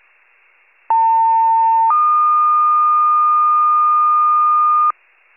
Two-tone page, heard as a one-second tone, then a three-second tone, about 900 then 1220 Hz. It is heard from 0.9 to 4.9 s.